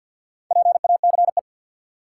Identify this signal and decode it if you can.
Morse code, heard as a keyed tone, about 52 wpm, 700 Hz. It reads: PACE
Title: Four Letter Words - 50wpm